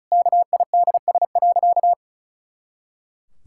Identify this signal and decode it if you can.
Morse code, heard as a keyed tone, about 36 wpm, 700 Hz. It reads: KIDS.
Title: Pangram Challenge v2 - 35wpm